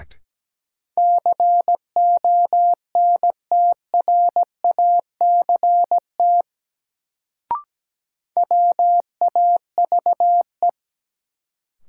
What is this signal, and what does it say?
Morse code, heard as a keyed tone, about 17 wpm, 700 Hz. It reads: CONTRACT WAVE
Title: Sentences from Top 1400 Words - New Word Drill - 17wpm